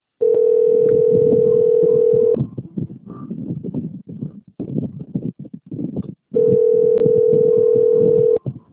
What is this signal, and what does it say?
Ringback tone (North America), 440 + 480 Hz.
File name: out- - - - - .80916